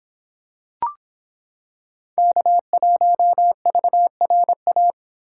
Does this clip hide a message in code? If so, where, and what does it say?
Yes: Morse code from 2.2 to 4.9 s, heard as a keyed tone, about 26 wpm, 700 Hz. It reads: K1VRA